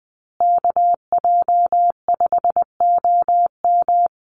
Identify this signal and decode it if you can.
Morse code, heard as a keyed tone, about 20 wpm, 700 Hz. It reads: KJ5OM
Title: USA Call Signs Repeated 2x 20wpm